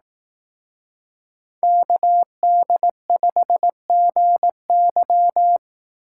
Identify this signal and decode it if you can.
Morse code, heard as a keyed tone, about 18 wpm, 700 Hz. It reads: KD5GY
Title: Speed Racing - USA 2x2 Call Signs 15wpm